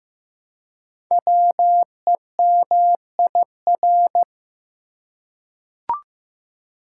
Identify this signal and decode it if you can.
Morse code, heard as a keyed tone, about 15 wpm, 700 Hz. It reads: WEMIR
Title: Code Groups - 5 Letters - 15wpm